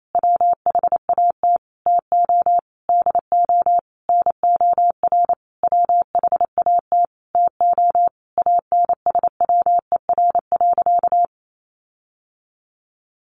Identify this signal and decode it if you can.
Morse code, heard as a keyed tone, about 28 wpm, 700 Hz. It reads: WHAT TO DO NOR WHAT TO ANSWER.
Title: Sentences from Top 1500 Words 28wpm